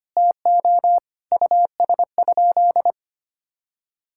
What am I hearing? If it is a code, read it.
Morse code, heard as a keyed tone, about 25 wpm, 700 Hz. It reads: TO US?
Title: Sentences from Top 1000 Words - Review Part 6 - 25wpm